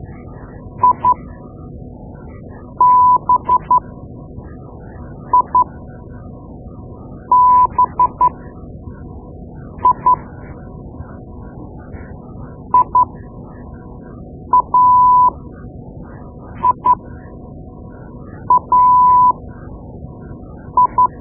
Morse code, heard as a keyed tone, about 12 wpm, spaced as for 5 wpm, 1000 Hz. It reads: IBIBI IAIAI
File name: Morse Lessons / Morse Lessons / Tape-2B